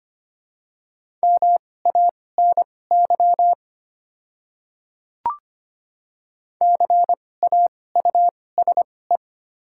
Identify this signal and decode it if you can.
Morse code, heard as a keyed tone, about 25 wpm, spaced as for 18 wpm, 690 Hz. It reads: MANY CAUSE
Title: 2x Character Spacing - Top 200 Words - 25wpm